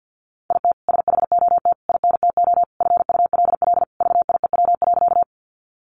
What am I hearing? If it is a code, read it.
Morse code, heard as a keyed tone, about 50 wpm, 700 Hz. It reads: IT SHOT INTO FULL VIEW.
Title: Sentences from Top 1100 Words 50wpm